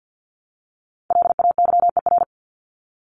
Morse code, heard as a keyed tone, about 48 wpm, 700 Hz. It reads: LAYER
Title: Top 1501-1750 Words 50wpm